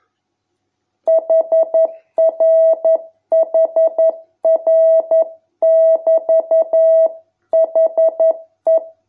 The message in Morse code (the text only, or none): HRHR<BT>HE